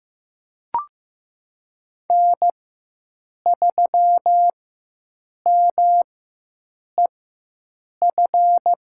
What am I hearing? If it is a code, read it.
Morse code, heard as a keyed tone, about 15 wpm, spaced as for 7 wpm, 700 Hz. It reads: N3MEF
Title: USA Call Signs - 4x Character Spacing - 15wpm